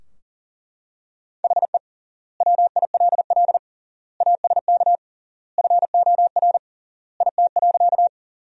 Morse code, heard as a keyed tone, about 40 wpm, 700 Hz. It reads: HE WILL ASK FOR IT.
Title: 3x Word-Spacing - Sentences from Top 200 Words - 40wpm